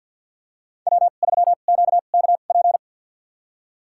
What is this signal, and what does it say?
Morse code, heard as a keyed tone, about 53 wpm, spaced as for 37 wpm, 700 Hz. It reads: W3QKP
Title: USA Call Signs - 5 Characters - 2x Character Spacing - 50wpm